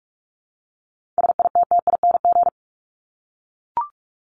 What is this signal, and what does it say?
Morse code, heard as a keyed tone, about 44 wpm, 700 Hz. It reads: SITTING
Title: ING Suffix - Top 100 Words with Base Words 45wpm